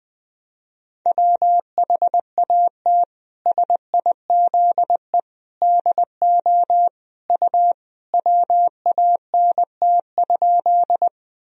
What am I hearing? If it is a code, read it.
Morse code, heard as a keyed tone, about 21 wpm, 700 Hz. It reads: WHAT SIZE DO U WANT?
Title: Sentences from Top 400 Words and Abbreviations 20wpm